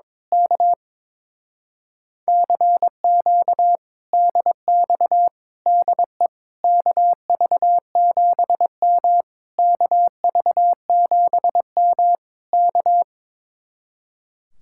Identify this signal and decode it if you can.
Morse code, heard as a keyed tone, about 22 wpm, 700 Hz. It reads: K CQ DX DE KV7M KV7M K
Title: Speed-Racing Common QSO Phrases - 22wpm